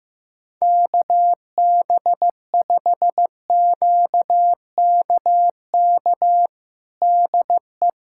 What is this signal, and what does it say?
Morse code, heard as a keyed tone, about 15 wpm, 700 Hz. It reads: KB5QKK DE
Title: USA Callsign DE Callsign - 6 Characters - 15wpm